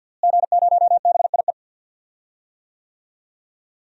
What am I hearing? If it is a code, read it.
Morse code, heard as a keyed tone, about 51 wpm, 700 Hz. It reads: G0BIE
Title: DXCC-100 Call Signs - 5 Characters - Repeated 3x - 50wpm